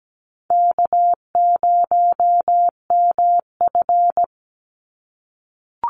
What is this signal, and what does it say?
Morse code, heard as a keyed tone, about 17 wpm, 700 Hz. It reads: K0MF
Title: USA Call Signs 17wpm